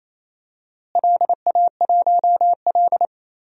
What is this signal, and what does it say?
Morse code, heard as a keyed tone, about 28 wpm, 700 Hz. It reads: LA1L
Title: DXCC-100 Call Signs - 4 Characters - Repeated 3x - 28wpm